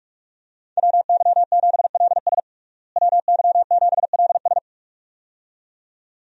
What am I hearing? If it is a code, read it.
Morse code, heard as a keyed tone, about 45 wpm, 700 Hz. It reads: WY7LS WY7LS
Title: USA Call Signs - 5 Characters - Repeated 2x - 45wpm